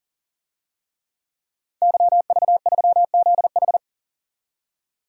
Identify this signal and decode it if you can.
Morse code, heard as a keyed tone, about 40 wpm, 700 Hz. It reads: YV3ZH
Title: DXCC-100 Call Signs - 5 Characters - 40wpm